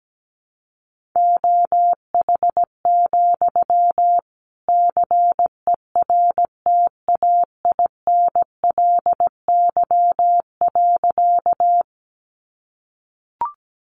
Morse code, heard as a keyed tone, about 17 wpm, 700 Hz. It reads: OH, CERTAINLY.
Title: Sentences from Top 900 Words 17wpm